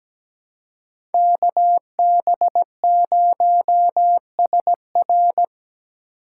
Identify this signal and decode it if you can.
Morse code, heard as a keyed tone, about 20 wpm, 700 Hz. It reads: KB0SR
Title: Speed Racing - USA 2x2 Call Signs 17wpm